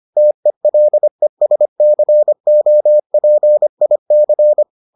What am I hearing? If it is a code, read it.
Morse code, heard as a keyed tone, about 25 wpm, 600 Hz. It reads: TELESCOPIC